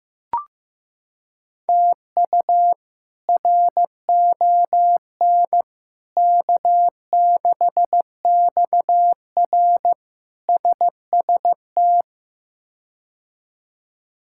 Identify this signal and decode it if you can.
Morse code, heard as a keyed tone, about 15 wpm, 700 Hz. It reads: TU RON K6XR SST
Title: K1USN SST QSO Practice - 15wpm